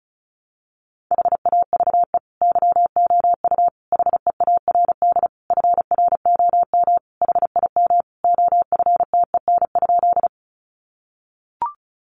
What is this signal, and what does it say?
Morse code, heard as a keyed tone, about 34 wpm, 700 Hz. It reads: HAVE YOU HEARD FROM HIM OFTEN?
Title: Sentences from Top 400 Words 35wpm